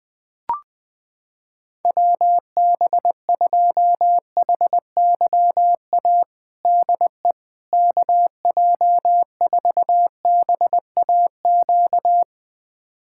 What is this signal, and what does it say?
Morse code, heard as a keyed tone, about 21 wpm, 700 Hz. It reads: WB2HYA DE KJ4BAQ
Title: USA Callsign DE Callsign - 6 Characters - 20wpm